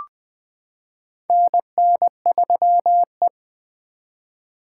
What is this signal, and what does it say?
Morse code, heard as a keyed tone, about 20 wpm, 700 Hz. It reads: NN3E
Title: 2023 POTA Activator Call Signs - 20wpm